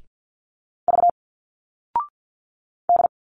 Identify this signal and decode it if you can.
Morse code, heard as a keyed tone, about 51 wpm, 700 Hz. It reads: V D
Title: Single Letters 50wpm